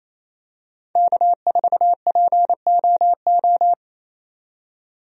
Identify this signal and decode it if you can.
Morse code, heard as a keyed tone, about 27 wpm, 700 Hz. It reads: K4POO
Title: USA Call Signs - 5 Characters - 28wpm